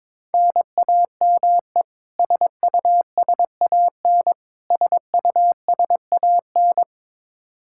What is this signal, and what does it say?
Morse code, heard as a keyed tone, about 22 wpm, 700 Hz. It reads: NAME SUSAN SUSAN
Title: Practice QSO - Exchange 1 - Beginning - 22wpm